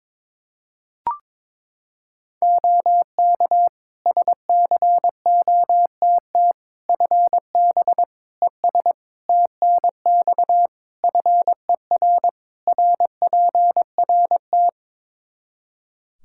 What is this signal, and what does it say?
Morse code, heard as a keyed tone, about 22 wpm, 700 Hz. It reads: OK SCOTT FB ES TNX FER RPRT
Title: Practice QSO - Exchange 2 - Info - 22wpm